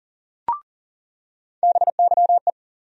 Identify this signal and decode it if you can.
Morse code, heard as a keyed tone, about 40 wpm, 700 Hz. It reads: BYE
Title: USA Call Sign Postfixes - 3 Letters 40wpm